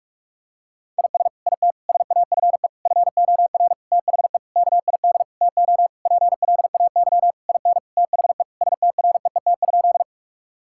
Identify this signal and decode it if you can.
Morse code, heard as a keyed tone, about 46 wpm, 700 Hz. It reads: IS IT SAFE FOR THE KID TO PLAY IN THE STREET?
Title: Sentences from Top 1600 Words 45wpm